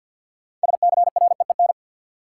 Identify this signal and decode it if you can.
Morse code, heard as a keyed tone, about 47 wpm, 700 Hz. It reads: SCREEN
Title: Six Letter Words - 50wpm